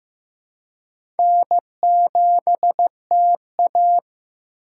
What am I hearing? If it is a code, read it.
Morse code, heard as a keyed tone, about 15 wpm, 700 Hz. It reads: N7TA
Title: USA Call Signs - 4 Characters - 2x Character Spacing - 15wpm